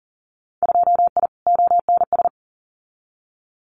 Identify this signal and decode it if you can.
Morse code, heard as a keyed tone, about 42 wpm, 700 Hz. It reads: 2I ONS
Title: Winter Field Day Exchanges Encore 40wpm